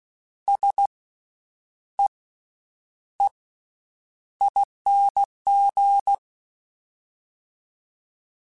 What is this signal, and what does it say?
Morse code, heard as a keyed tone, about 16 wpm, spaced as for 8 wpm, 800 Hz. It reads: SEEING
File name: Special INGs / Special INGs-mp3s / VerySlowToFaster Ing @15